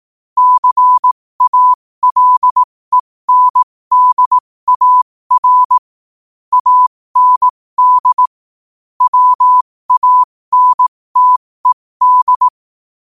Morse code, heard as a keyed tone, about 18 wpm, spaced as for 15 wpm, 1000 Hz. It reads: CALENDAR AND WANTED
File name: qotdfiles / 15wpm